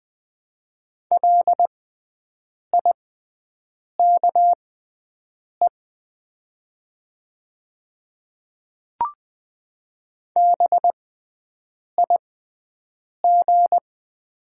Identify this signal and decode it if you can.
Morse code, heard as a keyed tone, about 20 wpm, spaced as for 7 wpm, 700 Hz. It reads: LIKE BIG
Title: 6x Character Spacing - Top 300 Words - 20wpm